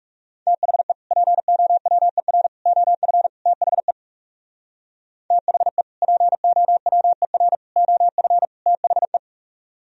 Morse code, heard as a keyed tone, about 36 wpm, 700 Hz. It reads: THE POWER OF THE THE POWER OF THE
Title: Speed-Racing - Sets of 4 Words 30wpm